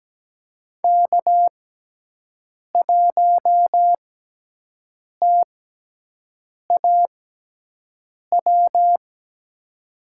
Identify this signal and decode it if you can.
Morse code, heard as a keyed tone, about 17 wpm, spaced as for 6 wpm, 700 Hz. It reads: K1TAW
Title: USA Call Signs - 5 Characters - 6x Character Spacing - 17wpm